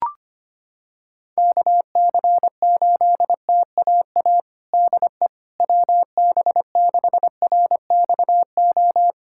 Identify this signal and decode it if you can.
Morse code, heard as a keyed tone, about 25 wpm, 700 Hz. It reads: KC8TAA DE WB6RXO
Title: USA Callsign DE Callsign - 6 Characters - 25wpm